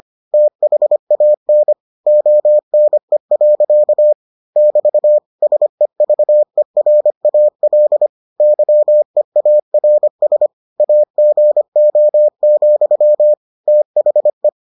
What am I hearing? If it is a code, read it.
Morse code, heard as a keyed tone, about 25 wpm, 600 Hz. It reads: THAN ONE. <BT> SEVERAL YEARS AGO, THE